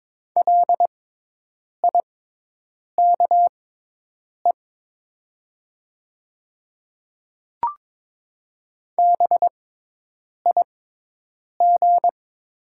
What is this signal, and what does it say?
Morse code, heard as a keyed tone, about 22 wpm, spaced as for 8 wpm, 700 Hz. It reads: LIKE BIG
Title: 6x Character Spacing - Top 300 Words - 22wpm